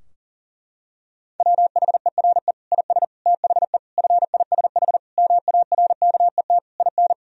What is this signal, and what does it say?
Morse code, heard as a keyed tone, about 40 wpm, 700 Hz. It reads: WHERE IS THE FISH MARKET IN
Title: Sentences from Top 1000 Words - Review Part 3 - 40wpm